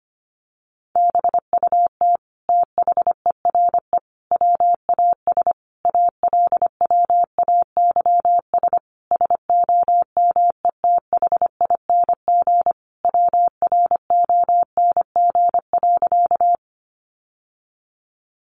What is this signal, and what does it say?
Morse code, heard as a keyed tone, about 25 wpm, 700 Hz. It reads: BUT THERE WAS ALWAYS SOMETHING WRONG.
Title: Sentences from Top 900 Words 25wpm